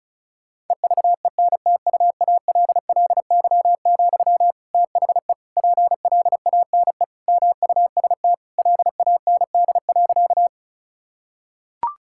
Morse code, heard as a keyed tone, about 36 wpm, 700 Hz. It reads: EVENTUALLY, THE PLANE MUST LAND.
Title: Sentences from Top 1000 Words - Review Part 6 - 35wpm